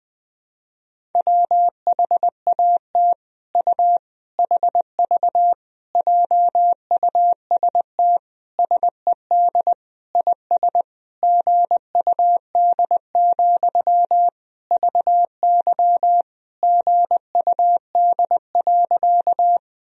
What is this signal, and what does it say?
Morse code, heard as a keyed tone, about 20 wpm, 700 Hz. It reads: WHAT U HV JUST SED IS GUD, VY GUD.
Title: Sentences from Top 200 Words and Abbreviations 20wpm